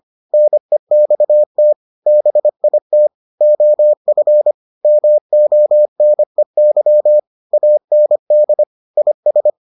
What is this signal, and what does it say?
Morse code, heard as a keyed tone, about 25 wpm, 600 Hz. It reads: NEXT BIT OF MONEY AND IS